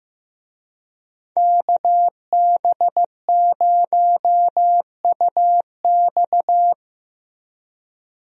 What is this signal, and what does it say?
Morse code, heard as a keyed tone, about 15 wpm, 700 Hz. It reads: KB0UX
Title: USA Call Signs - 5 Characters - 15wpm